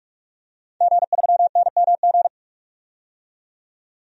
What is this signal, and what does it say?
Morse code, heard as a keyed tone, about 46 wpm, 700 Hz. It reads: G3NMG
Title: DXCC-100 Call Signs - 5 Characters - 45wpm